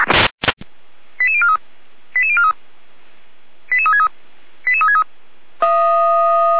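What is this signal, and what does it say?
Five-tone selective call (ZVEI), heard as quick runs of five tones, tones of about 70 ms. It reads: 88043 88043 88363 88363
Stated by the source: ZVEI five-tone selective call received on an SDR.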